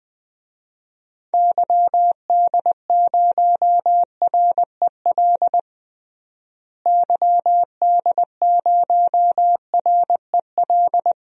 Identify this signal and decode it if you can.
Morse code, heard as a keyed tone, about 21 wpm, 700 Hz. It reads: YD0REL YD0REL
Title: DXCC-100 Call Signs - 6 Characters - Repeated 2x - 20wpm